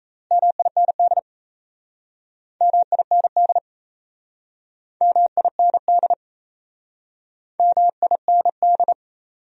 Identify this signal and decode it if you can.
Morse code, heard as a keyed tone, about 39 wpm, 700 Hz. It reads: MIND MIND MIND MIND